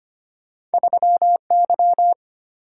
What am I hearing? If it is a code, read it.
Morse code, heard as a keyed tone, about 34 wpm, spaced as for 30 wpm, 700 Hz. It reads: 3Y
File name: practice / 25wpm / Speed-Racing - Two Letter-Number Combinations - 25wpm